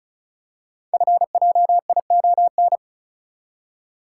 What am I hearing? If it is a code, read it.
Morse code, heard as a keyed tone, about 36 wpm, 690 Hz. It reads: FJION
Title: Code Groups - 5 Letters - 35wpm